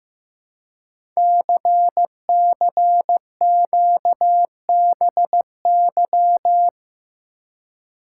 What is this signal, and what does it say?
Morse code, heard as a keyed tone, about 15 wpm, 700 Hz. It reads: CCQBY
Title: Code Groups - 5 Letters - 15wpm